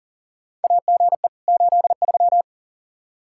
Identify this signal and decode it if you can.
Morse code, heard as a keyed tone, about 40 wpm, 700 Hz. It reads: AGE 83
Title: K1USN SST QSO Practice 40wpm